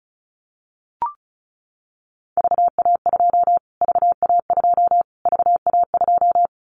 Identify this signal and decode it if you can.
Morse code, heard as a keyed tone, about 34 wpm, 700 Hz. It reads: VA2 VA2 VA2
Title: Callsign Prefixes for Top 25 DXCC - Repeated 3x - 35wpm